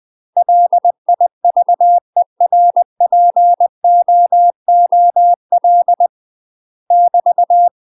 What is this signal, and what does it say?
Morse code, heard as a keyed tone, about 20 wpm, 700 Hz. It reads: LIVERPOOL <BT>
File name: UK MASTER QSOs TRAINING / 20WPM / UK MASTER QSOs 0042